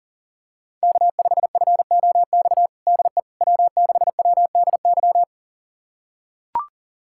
Morse code, heard as a keyed tone, about 40 wpm, 700 Hz. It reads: K5FOX DE W6WDY
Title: USA Callsign DE Callsign - 5 Characters - 40wpm